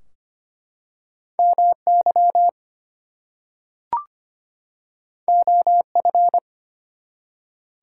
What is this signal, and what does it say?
Morse code, heard as a keyed tone, about 25 wpm, 700 Hz. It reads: MY OF
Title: Two Letter Words - 25wpm